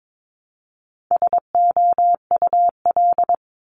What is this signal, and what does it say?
Morse code, heard as a keyed tone, about 22 wpm, 700 Hz. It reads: SOUL
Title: Multi-letter Phonemes Repeated 3x 22wpm